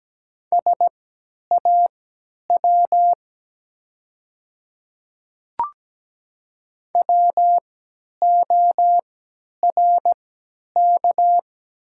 Morse code, heard as a keyed tone, about 17 wpm, spaced as for 10 wpm, 700 Hz. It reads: SAW WORK
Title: 3x Character Spacing - Top 300 Words - 17wpm